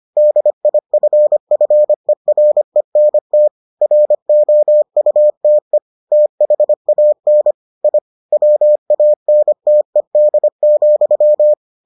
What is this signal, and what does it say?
Morse code, heard as a keyed tone, about 25 wpm, 600 Hz. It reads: DIFFERENT ROUTE THAN I WANTED,